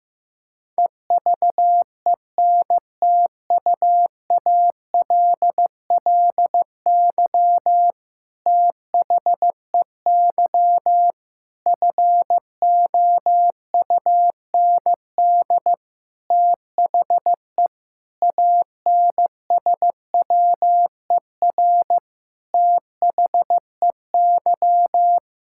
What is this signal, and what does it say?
Morse code, heard as a keyed tone, about 15 wpm, 700 Hz. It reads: EVENTUALLY THEY FOUND THE ANSWER THEY
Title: Sentences from Top 1000 Words - Review Part 6 - 15wpm